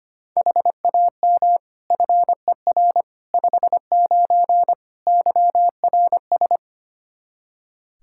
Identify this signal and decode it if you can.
Morse code, heard as a keyed tone, about 25 wpm, 700 Hz. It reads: HAM FER 59 YRS
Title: Practice QSO - Exchange 3 - More Info - 25wpm